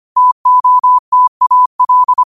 Morse code, heard as a keyed tone, about 25 wpm, 1000 Hz. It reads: TOTAL
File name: qotdfiles / 25wpm